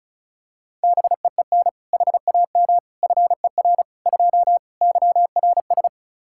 Morse code, heard as a keyed tone, about 36 wpm, 700 Hz. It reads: BEEN HAM FER 2 YRS
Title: K1USN SST QSO Practice 35wpm